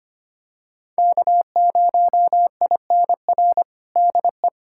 Morse code, heard as a keyed tone, about 25 wpm, 700 Hz. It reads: K0INR DE
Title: USA Callsign DE Callsign - 5 Characters - 25wpm